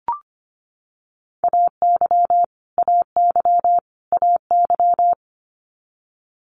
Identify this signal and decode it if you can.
Morse code, heard as a keyed tone, about 25 wpm, 700 Hz. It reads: AY AY AY
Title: Multi-letter Phonemes Repeated 3x 25wpm